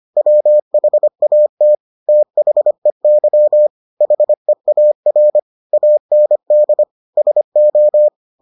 Morse code, heard as a keyed tone, about 25 wpm, 600 Hz. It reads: WHAT THEY HEAR AND SO